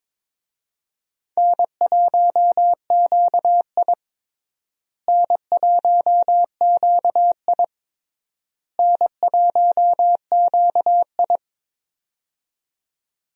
Morse code, heard as a keyed tone, about 22 wpm, 700 Hz. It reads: N1QI N1QI N1QI
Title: USA Call Signs - 4 Characters - Repeated 3x - 22wpm